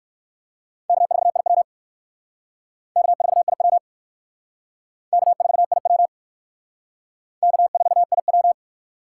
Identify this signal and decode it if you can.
Morse code, heard as a keyed tone, about 57 wpm, 700 Hz. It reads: K4IW K4IW K4IW K4IW